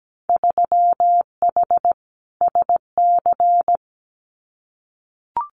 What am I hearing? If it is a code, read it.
Morse code, heard as a keyed tone, about 17 wpm, 700 Hz. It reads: 3H SC
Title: Winter Field Day Exchanges 17wpm